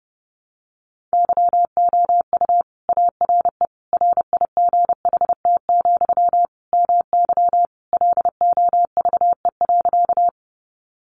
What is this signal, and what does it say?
Morse code, heard as a keyed tone, about 30 wpm, 700 Hz. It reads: YOU ARE RIGHT, MY LOVE.
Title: Sentences from Top 400 Words 30wpm